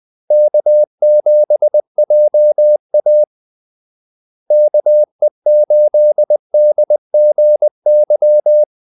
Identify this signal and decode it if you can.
Morse code, heard as a keyed tone, about 20 wpm, 600 Hz. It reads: K7JA KE8DGY
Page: Kaldesignaler 20 wpm